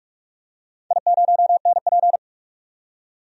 Morse code, heard as a keyed tone, about 42 wpm, 700 Hz. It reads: I0NP
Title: DXCC-100 Call Signs - 4 Characters - 45wpm